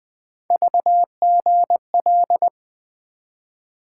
Morse code, heard as a keyed tone, about 20 wpm, 700 Hz. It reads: VGL
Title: Three Letter-Number Combinations - 20wpm